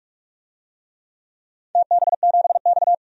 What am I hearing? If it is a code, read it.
Morse code, heard as a keyed tone, about 48 wpm, 700 Hz. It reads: TB7X
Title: DXCC-100 Call Signs - 4 Characters - 45wpm